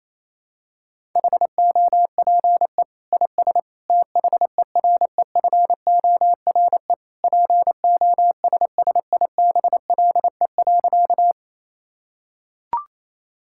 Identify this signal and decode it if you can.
Morse code, heard as a keyed tone, about 28 wpm, 700 Hz. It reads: HOPE IS THEREFORE POSSIBLE.
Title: Sentences from Top 1000 Words - Review Part 5 - 28wpm